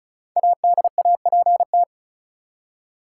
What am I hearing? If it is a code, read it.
Morse code, heard as a keyed tone, about 36 wpm, 700 Hz. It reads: ADAPT
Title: Five Letter Words - 35wpm